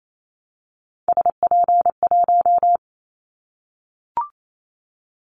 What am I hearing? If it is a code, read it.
Morse code, heard as a keyed tone, about 28 wpm, 700 Hz. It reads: SP1
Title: Callsign Prefixes for Top 25 DXCC - 28wpm